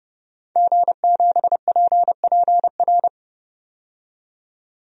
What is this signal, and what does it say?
Morse code, heard as a keyed tone, about 32 wpm, 700 Hz. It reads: G7PPR
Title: DXCC-100 Call Signs - 5 Characters - 30wpm